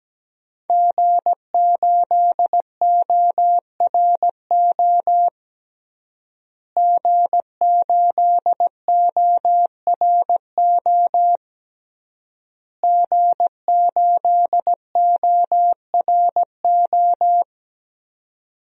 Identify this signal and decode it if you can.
Morse code, heard as a keyed tone, about 17 wpm, 700 Hz. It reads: G8ORO G8ORO G8ORO
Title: DXCC-100 Call Signs - 5 Characters - Repeated 3x - 17wpm